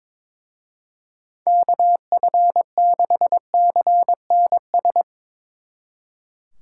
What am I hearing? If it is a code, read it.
Morse code, heard as a keyed tone, about 23 wpm, 700 Hz. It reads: KF6CNS
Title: Speed Racing - USA 2x3 Call Signs 22wpm